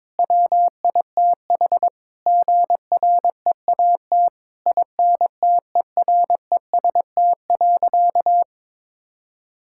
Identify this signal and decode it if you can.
Morse code, heard as a keyed tone, about 22 wpm, 700 Hz. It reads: WITH GREAT INTEREST.
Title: Sentences from Top 1000 Words - Review Part 6 - 22wpm